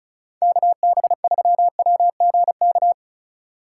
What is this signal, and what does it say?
Morse code, heard as a keyed tone, about 37 wpm, 700 Hz. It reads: KB3WGK
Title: Speed Racing - USA 2x3 Call Signs 30wpm